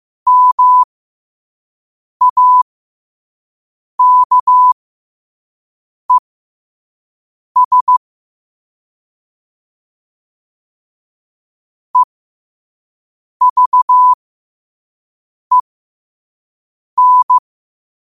Morse code, heard as a keyed tone, about 15 wpm, spaced as for 5 wpm, 1000 Hz. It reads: MAKES EVEN
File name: qotdfiles / 5wpm